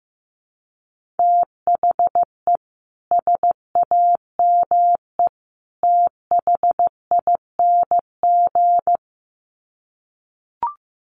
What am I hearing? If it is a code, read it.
Morse code, heard as a keyed tone, about 15 wpm, 700 Hz. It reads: THE SAME THING
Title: Sets of 3 Words 15wpm